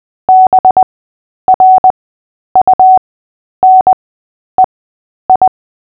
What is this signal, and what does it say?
Morse code, heard as a keyed tone, about 20 wpm, spaced as for 10 wpm, 750 Hz. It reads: BRUNEI
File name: cw / fists / Short-Story-3-10wpm